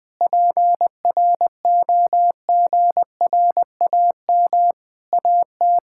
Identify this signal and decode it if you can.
Morse code, heard as a keyed tone, about 20 wpm, 700 Hz. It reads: PROGRAM AT
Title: Sentences from Top 1000 Words - Review Part 5 - 20wpm